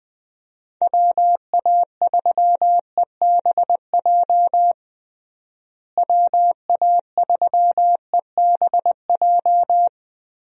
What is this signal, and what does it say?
Morse code, heard as a keyed tone, about 20 wpm, 700 Hz. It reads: WA3EBJ WA3EBJ
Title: USA Call Signs - 6 Characters - Repeated 3x - 20wpm